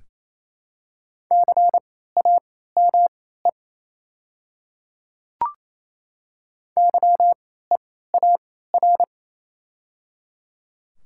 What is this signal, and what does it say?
Morse code, heard as a keyed tone, about 28 wpm, spaced as for 16 wpm, 700 Hz. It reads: CAME YEAR
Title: 3x Character Spacing - Top 500 Words - 28wpm